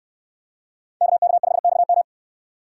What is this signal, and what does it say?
Morse code, heard as a keyed tone, about 68 wpm, 700 Hz. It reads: KK5XM